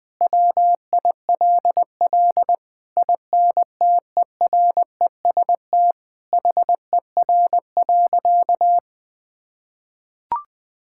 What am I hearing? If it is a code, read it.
Morse code, heard as a keyed tone, about 21 wpm, 700 Hz. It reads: WILL INTEREST HER.
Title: Sentences from Top 1000 Words - Review Part 5 - 20wpm